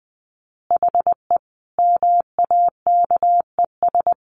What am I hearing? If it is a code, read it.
Morse code, heard as a keyed tone, about 20 wpm, 700 Hz. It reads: HE MAKES
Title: Additional Easy Phrases 20wpm